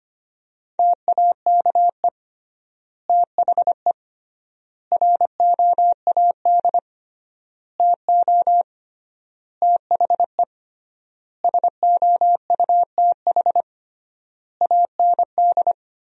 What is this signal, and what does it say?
Morse code, heard as a keyed tone, about 25 wpm, 700 Hz. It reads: TAKE THE ROAD TO THE SOUTH AND
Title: 3x Word-Spacing - Sentences from Top 400 Words - 25wpm